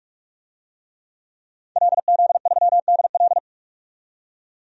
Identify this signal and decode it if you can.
Morse code, heard as a keyed tone, about 46 wpm, 700 Hz. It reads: LZ3DL
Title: DXCC-100 Call Signs - 5 Characters - 45wpm